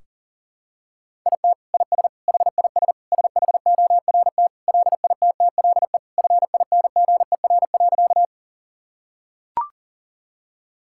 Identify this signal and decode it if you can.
Morse code, heard as a keyed tone, about 40 wpm, 700 Hz. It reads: IT IS HIS SHORT LITTLE FINGER.
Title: Sentences from Top 1600 Words 40wpm